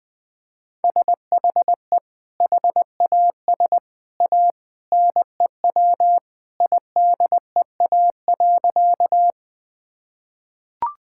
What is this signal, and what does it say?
Morse code, heard as a keyed tone, about 20 wpm, 700 Hz. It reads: SHE HAS A NEW IDEA.
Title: Sentences with 2-Word Phrases - 20wpm